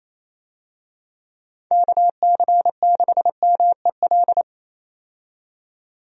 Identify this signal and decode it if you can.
Morse code, heard as a keyed tone, about 28 wpm, 700 Hz. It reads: KC6MEL
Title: USA Call Signs - 6 Characters - 28wpm